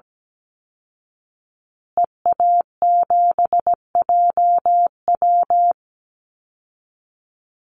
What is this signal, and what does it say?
Morse code, heard as a keyed tone, about 17 wpm, 700 Hz. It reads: EA7JW
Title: Call Signs Repeated 2x - 17wpm